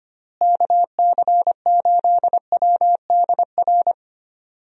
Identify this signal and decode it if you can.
Morse code, heard as a keyed tone, about 25 wpm, 700 Hz. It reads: KC8WDR
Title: USA Call Signs - 6 Characters - Repeated 2x - 25wpm